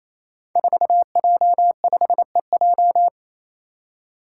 Morse code, heard as a keyed tone, about 28 wpm, 700 Hz. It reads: 4J5EJ
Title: Five Challenging Letter-Number Combinations 28wpm